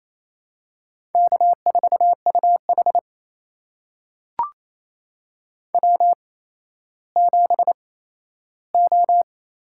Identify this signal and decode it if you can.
Morse code, heard as a keyed tone, about 28 wpm, spaced as for 14 wpm, 700 Hz. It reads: K4UH W7O
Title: USA Call Signs - 4 Characters - 8x Character Spacing - 28wpm